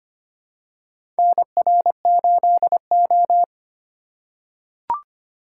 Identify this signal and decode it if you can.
Morse code, heard as a keyed tone, about 25 wpm, 700 Hz. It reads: NR8O
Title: USA Call Signs - 4 Characters - 4x Character Spacing - 25wpm